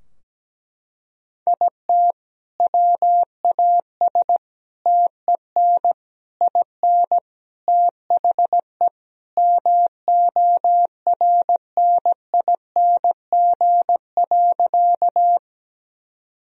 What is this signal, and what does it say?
Morse code, heard as a keyed tone, about 17 wpm, 700 Hz. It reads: IT WAS TEN IN THE MORNING.
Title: Sentences from Top 1000 Words - Review Part 2 - 17wpm